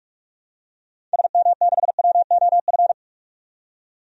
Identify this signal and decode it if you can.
Morse code, heard as a keyed tone, about 46 wpm, 700 Hz. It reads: SM6WOF
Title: DXCC-100 Call Signs - 6 Characters - 45wpm